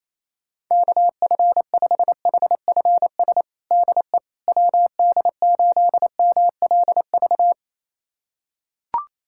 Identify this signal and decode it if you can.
Morse code, heard as a keyed tone, about 28 wpm, 700 Hz. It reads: KF5HFS DE WD8MLV
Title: USA Callsign DE Callsign - 6 Characters - 28wpm